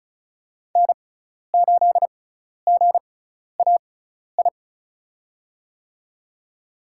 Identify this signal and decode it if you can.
Morse code, heard as a keyed tone, about 36 wpm, spaced as for 12 wpm, 700 Hz. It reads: N8GAI
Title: USA Call Signs - 5 Characters - 6x Character Spacing - 35wpm